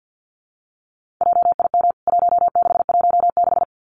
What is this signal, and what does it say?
Morse code, heard as a keyed tone, about 50 wpm, 700 Hz. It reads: JIM 1616
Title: CWT Exchange 50wpm